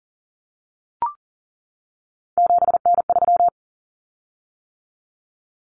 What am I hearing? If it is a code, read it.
Morse code, heard as a keyed tone, about 40 wpm, 700 Hz. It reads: 7N3
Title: Common Cut Numbers - 3-Digit 40wpm